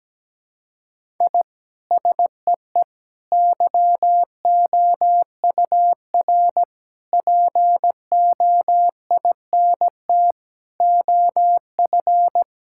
Morse code, heard as a keyed tone, about 17 wpm, 700 Hz. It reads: I SEE YOUR POINT OF
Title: Sentences from Top 1000 Words - Review Part 6 - 17wpm